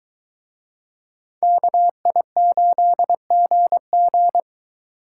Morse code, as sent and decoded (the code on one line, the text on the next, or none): -.- .. ---.. --. --.
KI8GG